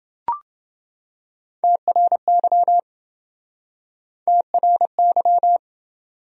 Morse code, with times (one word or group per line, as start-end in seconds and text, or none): 1.6-2.8 TRY
4.3-5.6 TRY